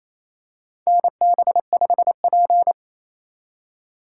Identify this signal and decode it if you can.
Morse code, heard as a keyed tone, about 28 wpm, 700 Hz. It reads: NB5P
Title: USA Call Signs - 4 Characters - 2x Character Spacing - 28wpm